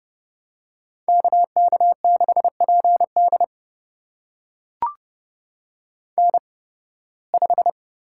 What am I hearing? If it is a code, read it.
Morse code, heard as a keyed tone, about 31 wpm, spaced as for 20 wpm, 700 Hz. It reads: KK6PD N5
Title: USA Call Signs - 5 Characters - 8x Character Spacing - 30wpm